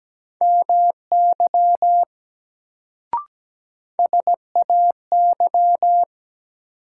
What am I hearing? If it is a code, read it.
Morse code, heard as a keyed tone, about 17 wpm, 700 Hz. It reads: MY SAY
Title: Sentences from Top 200 Words - New Word Drill - 17wpm